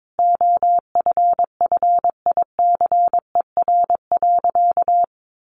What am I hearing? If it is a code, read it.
Morse code, heard as a keyed tone, about 22 wpm, 700 Hz. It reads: OFFICER.
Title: Sentences from Top 1200 Words 22wpm